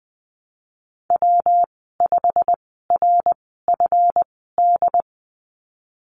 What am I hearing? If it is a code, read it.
Morse code, heard as a keyed tone, about 20 wpm, spaced as for 15 wpm, 700 Hz. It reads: W5RFD
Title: 231 Lesson 37g - Letter x - Call Signs 20wpm